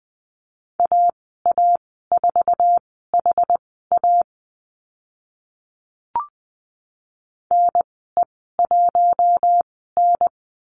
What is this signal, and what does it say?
Morse code, heard as a keyed tone, about 20 wpm, spaced as for 14 wpm, 700 Hz. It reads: AA4HA NE1N
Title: 069 Lesson 12g - Letter d - Call Signs 20wpm